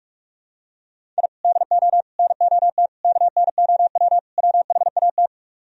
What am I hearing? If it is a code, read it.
Morse code, heard as a keyed tone, about 46 wpm, 700 Hz. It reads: I DO NOT KNOW WHAT